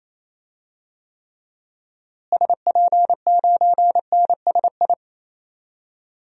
Morse code, heard as a keyed tone, about 28 wpm, 700 Hz. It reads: SP9NSI
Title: DXCC-100 Call Signs - 6 Characters - Repeated 2x - 28wpm